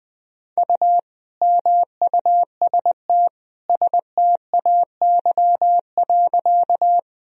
Morse code, heard as a keyed tone, about 21 wpm, 700 Hz. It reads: U MUST STAY.
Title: Sentences from Top 500 Words and Abbreviations 20wpm